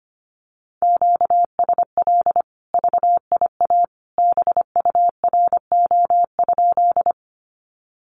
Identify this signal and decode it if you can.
Morse code, heard as a keyed tone, about 25 wpm, 700 Hz. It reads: QSL VIA BURO?
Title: Common QSO Phrases 25wpm